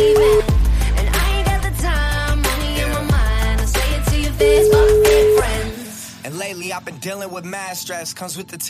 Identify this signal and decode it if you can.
Ringback tone (Europe, 425 Hz), with speech.